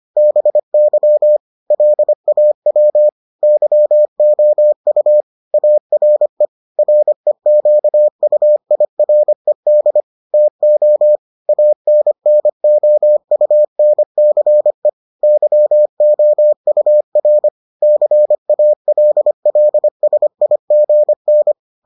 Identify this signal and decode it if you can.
Morse code, heard as a keyed tone, about 25 wpm, 600 Hz. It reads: BY LAW YOU ARE REQUIRED TO ANNOUNCE YOUR CALLSIGN